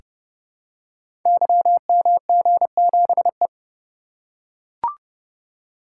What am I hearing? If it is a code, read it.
Morse code, heard as a keyed tone, about 30 wpm, 700 Hz. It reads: YMG7E
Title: Code Groups - 5 Letter:Numbers - 30wpm